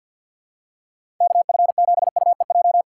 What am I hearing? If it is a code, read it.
Morse code, heard as a keyed tone, about 47 wpm, 700 Hz. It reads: KF7UEJ
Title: USA Call Signs - 6 Characters - 6x Character Spacing - 50wpm